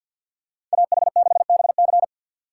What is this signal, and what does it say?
Morse code, heard as a keyed tone, about 53 wpm, 700 Hz. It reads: AH6BC
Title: USA Call Signs - 2x Character Spacing - 50wpm